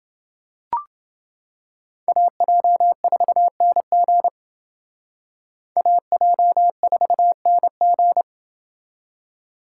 Morse code, heard as a keyed tone, about 28 wpm, 700 Hz. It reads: AJ4NG AJ4NG